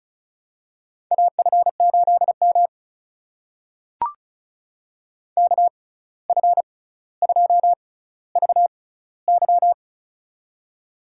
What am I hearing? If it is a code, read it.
Morse code, heard as a keyed tone, about 36 wpm, spaced as for 17 wpm, 700 Hz. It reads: AF8M KF2VY
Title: USA Call Signs - 6x Character Spacing - 35wpm